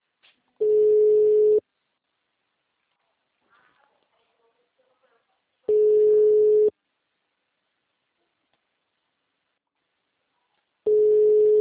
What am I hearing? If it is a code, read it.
Ringback tone (Europe, 425 Hz), with speech.